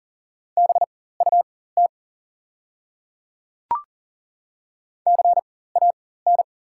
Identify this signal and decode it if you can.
Morse code, heard as a keyed tone, about 40 wpm, spaced as for 19 wpm, 690 Hz. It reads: BUT CAN
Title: 4x Character Spacing - Top 100 Words - 40wpm